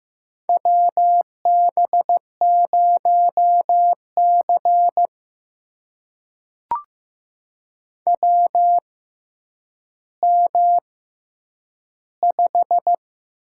Morse code, heard as a keyed tone, about 15 wpm, spaced as for 9 wpm, 700 Hz. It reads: WB0C WM5
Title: USA Call Signs - 4 Characters - 6x Character Spacing - 15wpm